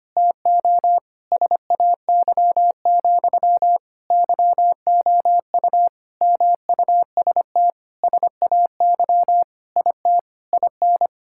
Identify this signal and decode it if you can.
Morse code, heard as a keyed tone, about 25 wpm, 700 Hz. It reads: TO SAY, YOU MUST SAY IT IN
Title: Sentences from Top 1600 Words 25wpm